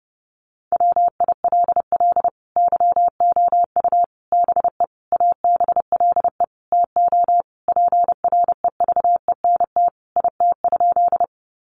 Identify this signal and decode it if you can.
Morse code, heard as a keyed tone, about 30 wpm, 700 Hz. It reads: WILL YOU BE ABLE TO PREVENT IT?
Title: Sentences from Top 1100 Words 30wpm